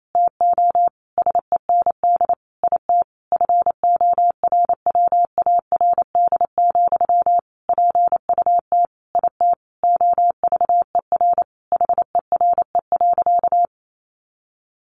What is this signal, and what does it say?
Morse code, heard as a keyed tone, about 28 wpm, 700 Hz. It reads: TO SEND IT FORWARD, PUT IT OVER HERE.
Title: Sentences from Top 1500 Words 28wpm